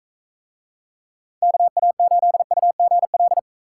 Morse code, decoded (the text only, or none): KA8UGL